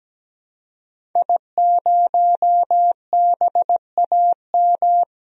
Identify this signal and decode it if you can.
Morse code, heard as a keyed tone, about 17 wpm, 700 Hz. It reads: I0BAM
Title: DXCC-100 Call Signs - 5 Characters - Repeated 3x - 17wpm